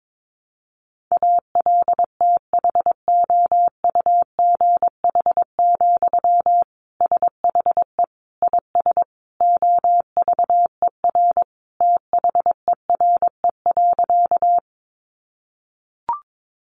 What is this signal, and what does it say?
Morse code, heard as a keyed tone, about 22 wpm, 700 Hz. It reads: ALTHOUGH, SHE IS OVER THERE.
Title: Sentences from Top 600 Words 22wpm